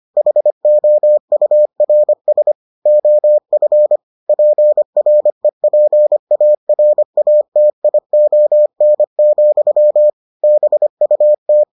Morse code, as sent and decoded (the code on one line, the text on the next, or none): .... --- ..- .-. ... / --- ..-. / .--. .-. .--. .- .-. .- - .. --- -. --..-- / -... ..- -
HOURS OF PREPARATION, BUT